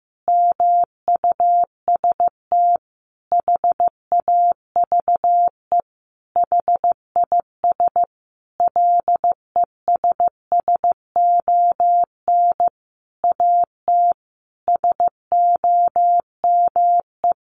Morse code, heard as a keyed tone, about 15 wpm, 700 Hz. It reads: MUST HAVE HIS LESSON AT SOME
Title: Sentences from Top 1200 Words 15wpm